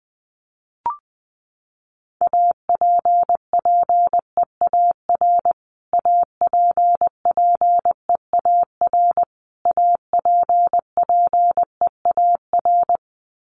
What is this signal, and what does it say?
Morse code, heard as a keyed tone, about 19 wpm, 700 Hz. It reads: APPEAR APPEAR APPEAR
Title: Top 500 Words Repeated 20wpm